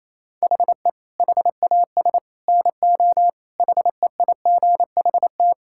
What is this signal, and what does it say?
Morse code, heard as a keyed tone, about 27 wpm, 700 Hz. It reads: HE HAS NO HEIGHT
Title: Sentences from Top 1600 Words 28wpm